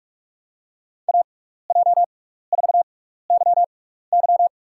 Morse code, heard as a keyed tone, about 43 wpm, spaced as for 15 wpm, 700 Hz. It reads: AJ4YY
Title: USA Call Signs - 5 Characters - 6x Character Spacing - 45wpm